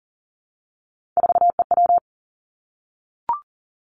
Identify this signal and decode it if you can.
Morse code, heard as a keyed tone, about 38 wpm, 690 Hz. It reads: 4EW
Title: Three Challenging Letter-Number Combinations 40wpm